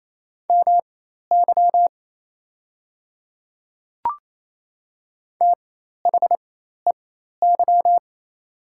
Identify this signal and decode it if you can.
Morse code, heard as a keyed tone, about 27 wpm, spaced as for 13 wpm, 700 Hz. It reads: MY THEY
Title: 4x Character Spacing - Top 100 Words - 28wpm